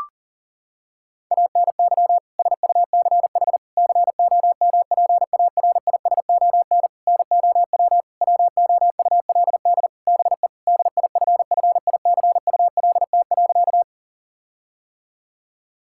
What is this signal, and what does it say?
Morse code, heard as a keyed tone, about 37 wpm, 700 Hz. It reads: ANY SUCH COMPARISON NOW WOULD BE DIFFICULT.
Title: Sentences from Top 1600 Words 40wpm